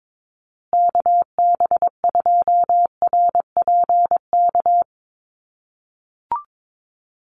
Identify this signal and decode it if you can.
Morse code, heard as a keyed tone, about 22 wpm, 700 Hz. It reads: KB2RPK
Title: USA Call Signs Repeated 2x 22wpm